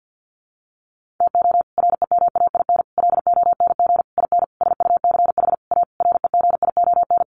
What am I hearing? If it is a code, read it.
Morse code, heard as a keyed tone, about 48 wpm, 700 Hz. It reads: TO REMAIN LONG IN SUCH A REGION
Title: Sentences from Top 1500 Words 50wpm